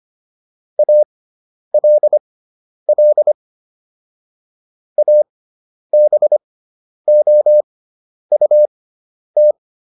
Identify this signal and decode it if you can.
Morse code, heard as a keyed tone, about 25 wpm, spaced as for 10 wpm, 600 Hz. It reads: ALL ABOUT